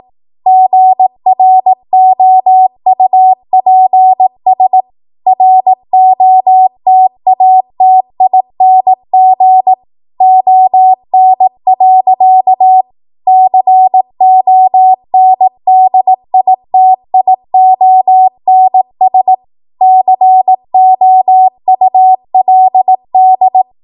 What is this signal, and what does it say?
Morse code, heard as a keyed tone, about 18 wpm, 750 Hz. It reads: GROUPS ROTATING ON. CONDITIONS COULD
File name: Morse / 18WPM